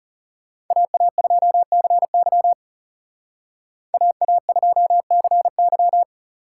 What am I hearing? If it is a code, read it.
Morse code, heard as a keyed tone, about 36 wpm, 700 Hz. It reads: AA2CY AA2CY
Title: Speed Racing - USA 2x2 Call Signs 30wpm